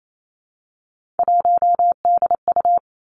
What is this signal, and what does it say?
Morse code, heard as a keyed tone, about 28 wpm, 690 Hz. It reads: 1DU
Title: Three Challenging Letter-Number Combinations 28wpm